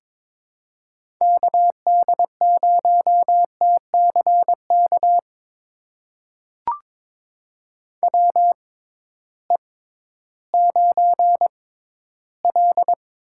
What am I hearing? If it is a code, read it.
Morse code, heard as a keyed tone, about 22 wpm, spaced as for 13 wpm, 700 Hz. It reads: KD0TCK WE9L
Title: USA Call Signs - 6x Character Spacing - 22wpm